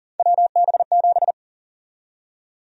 Morse code, heard as a keyed tone, about 40 wpm, 700 Hz. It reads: WB7
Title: USA Call Sign Prefixes - Repeated 2x - 40wpm